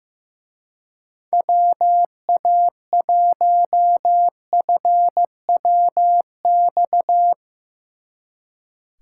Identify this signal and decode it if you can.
Morse code, heard as a keyed tone, about 15 wpm, 700 Hz. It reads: WA1FWX
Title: Speed Racing - USA 2x3 Call Signs 15wpm